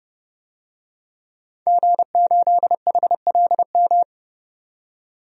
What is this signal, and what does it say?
Morse code, heard as a keyed tone, about 32 wpm, 700 Hz. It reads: G8HLM
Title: DXCC-100 Call Signs - 5 Characters - 30wpm